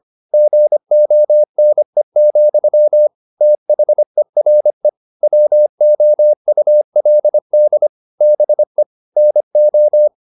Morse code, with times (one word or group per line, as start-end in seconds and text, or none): 0.3-3.1 GONE,
3.4-4.9 THERE
5.2-7.9 WOULD
8.2-8.8 BE
9.2-10.1 NO